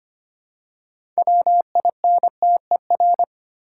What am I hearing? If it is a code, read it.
Morse code, heard as a keyed tone, about 25 wpm, 700 Hz. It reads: WINTER
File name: practice / 25wpm / Easy Words with Hints - 25wpm